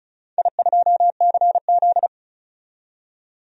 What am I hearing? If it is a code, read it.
Morse code, heard as a keyed tone, about 36 wpm, 700 Hz. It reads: I2CZ
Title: DXCC-100 Call Signs - 4 Characters - 35wpm